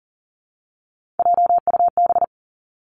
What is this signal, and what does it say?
Morse code, heard as a keyed tone, about 38 wpm, 700 Hz. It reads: JUB
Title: Three Challenging Letter-Number Combinations 40wpm